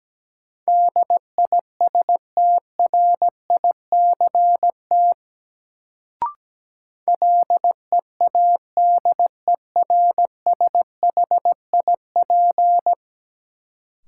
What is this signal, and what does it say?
Morse code, heard as a keyed tone, about 17 wpm, 700 Hz. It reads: DISTRICT LEADERSHIP
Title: Sentences from Top 1600 Words - New Word Drill - 17wpm